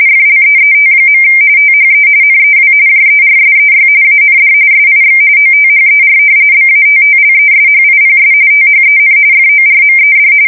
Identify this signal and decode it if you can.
RTTY, about 45.45 baud. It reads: FLEET. THE NEW JERSEY AND SIX OTHER BATTLESHIPS THEN FORMED
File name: fsk170 / battleshipnj